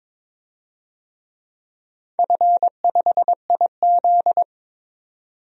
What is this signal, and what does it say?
Morse code, heard as a keyed tone, about 22 wpm, 700 Hz. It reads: F5IZ
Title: DXCC-100 Call Signs - 4 Characters - 22wpm